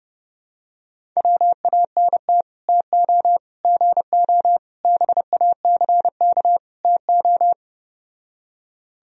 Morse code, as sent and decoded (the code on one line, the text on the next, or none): .-- .- -. - / - --- / --. --- / -... .- -.-. -.- / - ---
WANT TO GO BACK TO